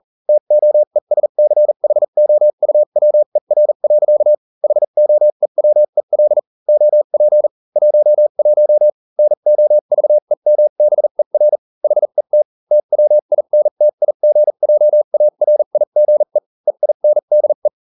Morse code, heard as a keyed tone, about 40 wpm, spaced as for 35 wpm, 600 Hz. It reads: TOESCHOUWER. HOEWEL OP 11 NOVEMBER HET TWINTIGJARIGE EINDE